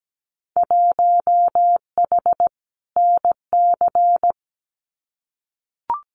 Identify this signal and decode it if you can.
Morse code, heard as a keyed tone, about 17 wpm, 700 Hz. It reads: 1H NC
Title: Winter Field Day Exchanges 17wpm